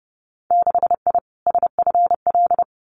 Morse code, heard as a keyed tone, about 30 wpm, 700 Hz. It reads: 6I SFL
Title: Winter Field Day Exchanges Encore 30wpm